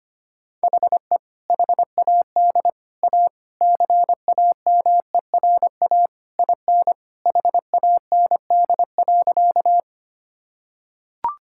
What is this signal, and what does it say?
Morse code, heard as a keyed tone, about 25 wpm, 700 Hz. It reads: HE HAD A CAMERA IN HAND.
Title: Sentences from Top 1000 Words - Review Part 5 - 25wpm